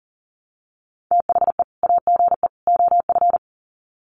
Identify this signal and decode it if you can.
Morse code, heard as a keyed tone, about 40 wpm, 700 Hz. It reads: THE AGE OF
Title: Sets of 3 Words 40wpm